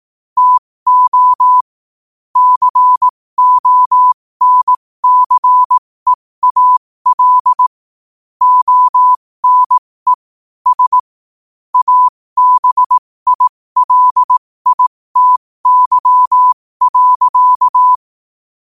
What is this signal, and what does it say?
Morse code, heard as a keyed tone, about 18 wpm, spaced as for 15 wpm, 1000 Hz. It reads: TO CONCEAL ONE S ABILITY.
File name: qotdfiles / 15wpm